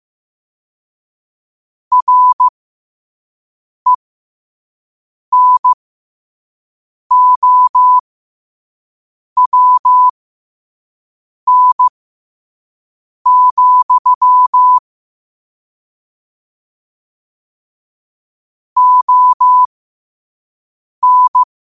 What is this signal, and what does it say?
Morse code, heard as a keyed tone, about 15 wpm, spaced as for 5 wpm, 1000 Hz. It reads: RENOWN, ON
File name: qotdfiles / 5wpm